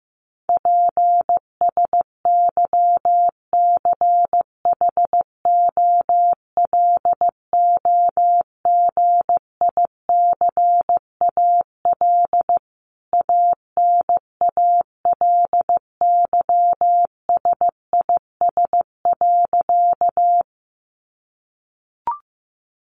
Morse code, heard as a keyed tone, about 15 wpm, 700 Hz. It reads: PSYCHOLOGICAL ANALYSIS.
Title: Sentences from Top 1400 Words 15wpm